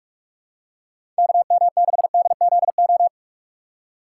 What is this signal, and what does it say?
Morse code, heard as a keyed tone, about 46 wpm, 700 Hz. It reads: KM6DZO
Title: USA Call Signs - 6 Characters - 2x Character Spacing - 45wpm